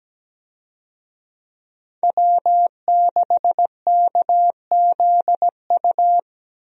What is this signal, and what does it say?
Morse code, heard as a keyed tone, about 17 wpm, 700 Hz. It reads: W6KZU
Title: USA Call Signs - 5 Characters - 17wpm